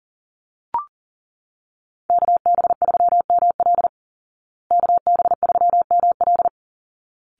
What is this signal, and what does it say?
Morse code, heard as a keyed tone, about 40 wpm, 700 Hz. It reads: KB3ML KB3ML
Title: Call Signs Repeated 3x - 40wpm